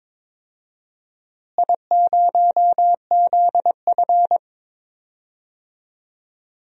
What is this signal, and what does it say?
Morse code, heard as a keyed tone, about 22 wpm, 700 Hz. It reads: I0ZF
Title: DXCC-100 Call Signs - 4 Characters - Repeated 2x - 22wpm